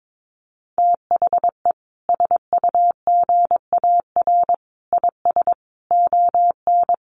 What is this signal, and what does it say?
Morse code, heard as a keyed tone, about 22 wpm, 700 Hz. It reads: THE SUGAR IS ON
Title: Sentences from Top 1200 Words 22wpm